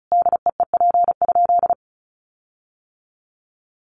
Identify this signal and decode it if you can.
Morse code, heard as a keyed tone, about 34 wpm, 700 Hz. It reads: DEEP?
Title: Sentences from Top 1300 Words 35wpm